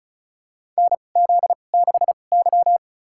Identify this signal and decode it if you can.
Morse code, heard as a keyed tone, about 34 wpm, spaced as for 25 wpm, 690 Hz. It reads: NZ6Y
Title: USA Call Signs - 4 Characters - 2x Character Spacing - 35wpm